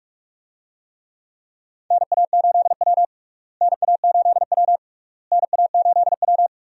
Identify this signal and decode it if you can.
Morse code, heard as a keyed tone, about 46 wpm, 700 Hz. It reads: NA8W NA8W NA8W
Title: USA Call Signs - 4 Characters - Repeated 3x - 45wpm